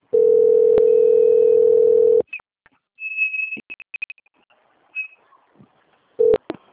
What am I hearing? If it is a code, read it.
Ringback tone (North America), 440 + 480 Hz.